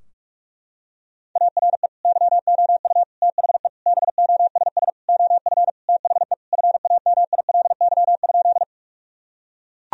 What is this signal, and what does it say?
Morse code, heard as a keyed tone, about 41 wpm, 700 Hz. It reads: ARE YOU THE BOSS OF THE FAMILY?
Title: Sentences from Top 1000 Words - Review Part 4 - 45wpm